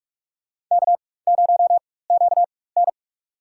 Morse code, heard as a keyed tone, about 46 wpm, spaced as for 21 wpm, 700 Hz. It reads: K0QN
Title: USA Call Signs - 4 Characters - 4x Character Spacing - 45wpm